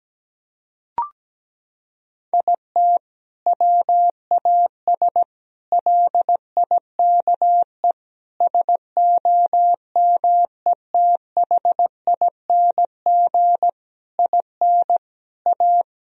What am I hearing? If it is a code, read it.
Morse code, heard as a keyed tone, about 17 wpm, 700 Hz. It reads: IT WAS LIKE SOMETHING IN A
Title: Sentences from Top 1000 Words - Review Part 1 - 17wpm